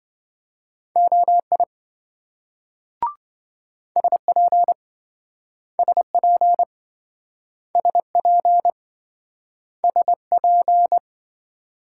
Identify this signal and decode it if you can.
Morse code, heard as a keyed tone, about 26 wpm, 700 Hz. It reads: OI SP SP SP SP